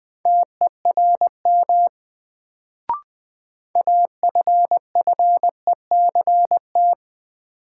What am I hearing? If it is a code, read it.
Morse code, heard as a keyed tone, about 20 wpm, 700 Hz. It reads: TERM AFFECT
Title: Sentences from Top 900 Words - New Word Drill - 20wpm